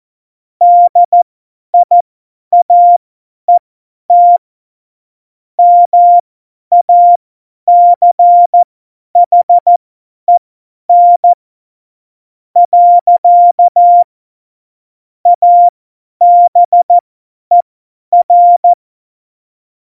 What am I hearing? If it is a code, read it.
Morse code, heard as a keyed tone, about 14 wpm, spaced as for 10 wpm, 700 Hz. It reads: DIAET MACHEN . ABER
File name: Morsekurs / Witze / Witz 13 March 2018, 50 F70